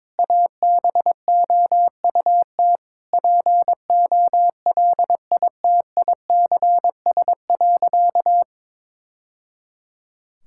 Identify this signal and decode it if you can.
Morse code, heard as a keyed tone, about 22 wpm, 700 Hz. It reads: ABOUT POLITICS.
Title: Sentences from Top 1000 Words - Review Part 6 - 22wpm